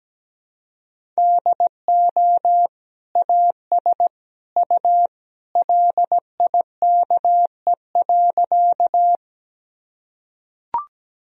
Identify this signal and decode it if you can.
Morse code, heard as a keyed tone, about 17 wpm, 700 Hz. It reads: DO AS U LIKE.
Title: Sentences from Top 100 Words and Abbreviations 17wpm